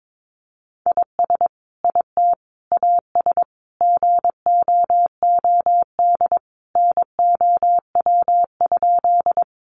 Morse code, heard as a keyed tone, about 22 wpm, 700 Hz. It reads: IS IT AS GOOD NOW?
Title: Sentences from Top 200 Words 22wpm